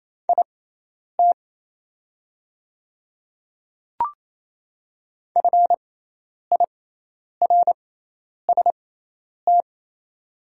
Morse code, heard as a keyed tone, about 28 wpm, spaced as for 10 wpm, 690 Hz. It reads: IT FIRST
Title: 6x Character Spacing - Top 100 Words - 28wpm